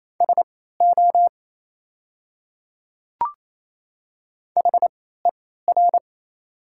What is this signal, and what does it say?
Morse code, heard as a keyed tone, about 27 wpm, spaced as for 16 wpm, 690 Hz. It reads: SO HER
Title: 3x Character Spacing - Top 200 Words - 28wpm